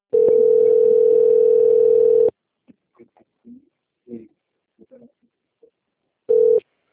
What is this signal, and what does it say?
Ringback tone (North America), 440 + 480 Hz.
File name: out- - - - - .23149